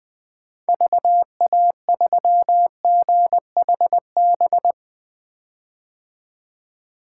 Morse code, heard as a keyed tone, about 20 wpm, 700 Hz. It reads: VA3GHB
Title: 2023 POTA Activator Call Signs - 20wpm